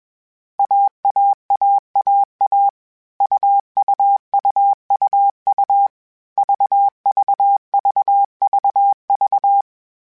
Morse code, heard as a keyed tone, about 21 wpm, 800 Hz. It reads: AAAAA UUUUU VVVVV
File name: CW Code Groups-mp3s / Sets mp3s / AUV AUV4 sp @20